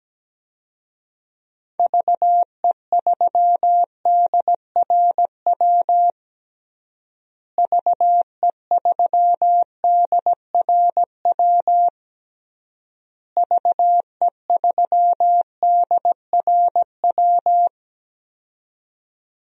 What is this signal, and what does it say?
Morse code, heard as a keyed tone, about 17 wpm, 700 Hz. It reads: VE3DRW VE3DRW VE3DRW
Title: DXCC-100 Call Signs - 6 Characters - Repeated 3x - 17wpm